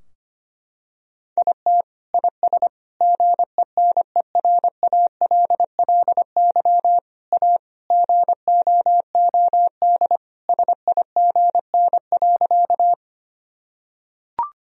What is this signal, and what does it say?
Morse code, heard as a keyed tone, about 25 wpm, 700 Hz. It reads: IT IS GENERALLY A GOOD SIGN.
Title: Sentences from Top 1000 Words - Review Part 4 - 25wpm